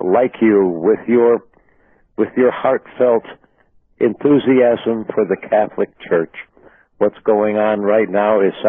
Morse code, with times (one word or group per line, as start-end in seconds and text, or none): none